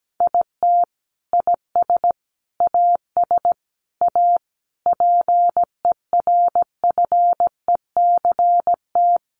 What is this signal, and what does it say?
Morse code, heard as a keyed tone, about 17 wpm, 700 Hz. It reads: IT IS AS A PERFECT
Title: Sentences from Top 1000 Words 17wpm